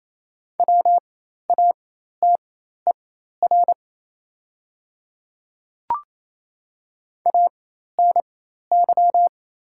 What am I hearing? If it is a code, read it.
Morse code, heard as a keyed tone, about 28 wpm, spaced as for 13 wpm, 690 Hz. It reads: WATER ANY
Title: 4x Character Spacing - Top 200 Words - 28wpm